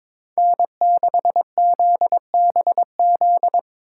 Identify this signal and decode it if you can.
Morse code, heard as a keyed tone, about 25 wpm, 700 Hz. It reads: N6ZBZ
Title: Speed Racing - USA 1x3 Call Signs 22wpm